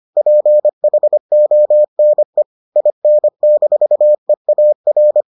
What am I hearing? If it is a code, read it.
Morse code, heard as a keyed tone, about 25 wpm, 600 Hz. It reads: PHONE IN-EAR